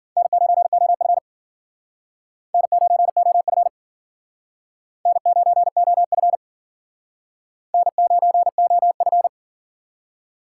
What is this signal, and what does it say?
Morse code, heard as a keyed tone, about 54 wpm, 700 Hz. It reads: N9OF N9OF N9OF N9OF